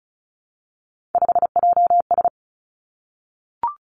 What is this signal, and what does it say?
Morse code, heard as a keyed tone, about 36 wpm, 700 Hz. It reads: 5JS
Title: Three Challenging Letter-Number Combinations 35wpm